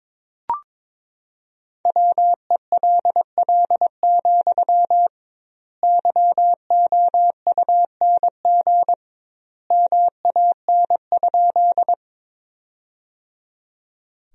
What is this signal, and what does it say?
Morse code, heard as a keyed tone, about 22 wpm, 700 Hz. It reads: WELL, YOUNG MAN?
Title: 2x Word-Spacing - Sentences from Top 400 Words - 22wpm